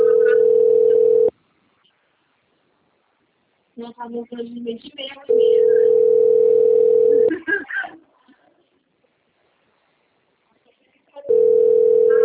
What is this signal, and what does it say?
Ringback tone (North America), 440 + 480 Hz, with speech.
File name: out- - - - - .95309